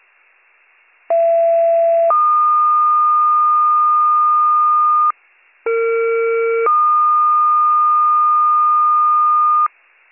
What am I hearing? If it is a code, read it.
Two-tone page, heard as a one-second tone, then a three-second tone, about 670 then 1220 Hz, about 460 then 1220 Hz. It is heard from 1.1 to 9.7 s.